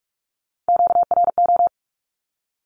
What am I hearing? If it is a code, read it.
Morse code, heard as a keyed tone, about 46 wpm, 700 Hz. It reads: QRO
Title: Q-Codes with Spoken Meaning - 45wpm